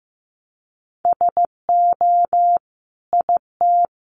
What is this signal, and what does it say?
Morse code, heard as a keyed tone, about 15 wpm, 700 Hz. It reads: SO IT
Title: Sentences from Top 100 Words 15wpm